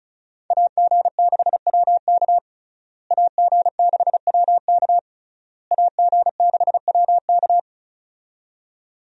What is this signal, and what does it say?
Morse code, heard as a keyed tone, about 34 wpm, 700 Hz. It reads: AG6WK AG6WK AG6WK
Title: USA Call Signs - 5 Characters - Repeated 3x - 35wpm